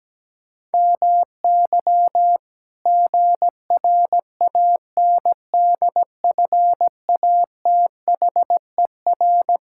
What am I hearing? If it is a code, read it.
Morse code, heard as a keyed tone, about 17 wpm, 700 Hz. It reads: MY GRANDFATHER
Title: Pangram Challenge v2 - 17wpm